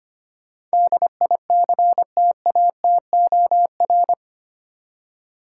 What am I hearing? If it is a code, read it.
Morse code, heard as a keyed tone, about 25 wpm, 700 Hz. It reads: DICTATOR
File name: Eight Letter Words - 25wpm